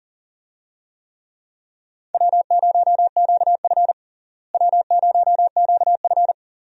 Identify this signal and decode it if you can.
Morse code, heard as a keyed tone, about 40 wpm, 700 Hz. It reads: W0QF W0QF
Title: USA Call Signs - 4 Characters - Repeated 2x - 40wpm